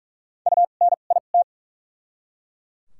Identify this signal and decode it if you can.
Morse code, heard as a keyed tone, about 43 wpm, spaced as for 31 wpm, 700 Hz. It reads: UNIT
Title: 2x Character Spacing - Top 500 Words - 45wpm